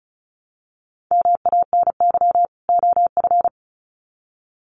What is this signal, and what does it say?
Morse code, heard as a keyed tone, about 34 wpm, 700 Hz. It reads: MANY OF
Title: Sets of 2 Words 35wpm